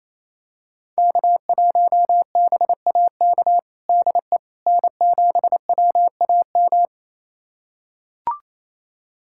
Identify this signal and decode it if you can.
Morse code, heard as a keyed tone, about 28 wpm, 700 Hz. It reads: K1BAK DE N7WAM
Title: USA Callsign DE Callsign - 5 Characters - 28wpm